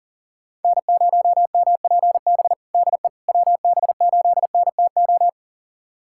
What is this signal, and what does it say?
Morse code, heard as a keyed tone, about 40 wpm, 700 Hz. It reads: N0MPB DE WB8NTO
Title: USA Callsign DE Callsign 40wpm